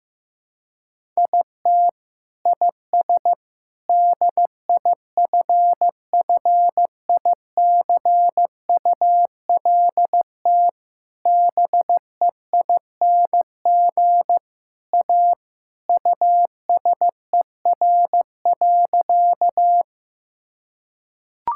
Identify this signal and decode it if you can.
Morse code, heard as a keyed tone, about 15 wpm, 700 Hz. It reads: IT IS DIFFICULT BEING A USER.
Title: Sentences from Top 1000 Words - Review Part 5 - 15wpm